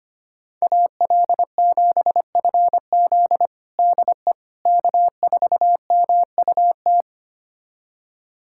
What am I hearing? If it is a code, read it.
Morse code, heard as a keyed tone, about 25 wpm, 700 Hz. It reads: AL7FZ DE K4MUT
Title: USA Callsign DE Callsign - 5 Characters - 25wpm